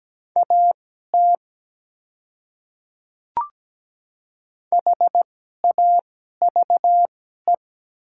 Morse code, heard as a keyed tone, about 17 wpm, spaced as for 12 wpm, 700 Hz. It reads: AT HAVE
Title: 2x Character Spacing - Top 100 Words - 17wpm